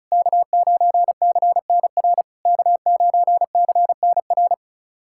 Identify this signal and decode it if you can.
Morse code, heard as a keyed tone, about 34 wpm, 690 Hz. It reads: K9CNR K9CNR
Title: SOTA QSO Practice 35wpm